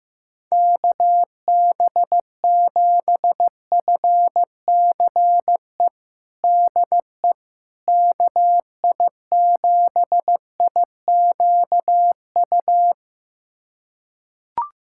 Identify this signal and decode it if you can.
Morse code, heard as a keyed tone, about 15 wpm, 700 Hz. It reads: KB7FCE DE KI7IQU
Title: USA Callsign DE Callsign - 6 Characters - 15wpm